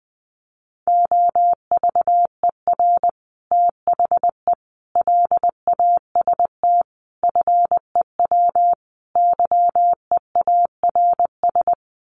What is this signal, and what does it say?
Morse code, heard as a keyed tone, about 20 wpm, 700 Hz. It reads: OVER THE LAST FEW YEARS
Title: Sets of 5 Words Encore 20wpm